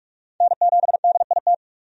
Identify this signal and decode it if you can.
Morse code, heard as a keyed tone, about 41 wpm, 700 Hz. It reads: N7DIT
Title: USA Call Signs - 5 Characters - 45wpm